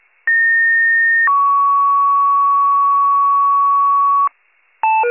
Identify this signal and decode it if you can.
Two-tone page, heard as a one-second tone, then a three-second tone, about 1850 then 1120 Hz. It is heard from 0.3 to 4.3 s.